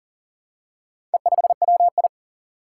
Morse code, heard as a keyed tone, about 40 wpm, 700 Hz. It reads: E5WI
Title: Four Challenging Letter-Number Combinations 40wpm